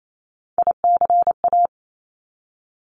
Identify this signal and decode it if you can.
Morse code, heard as a keyed tone, about 28 wpm, 700 Hz. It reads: ICA
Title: Top Three Letter Combinations in Words 28wpm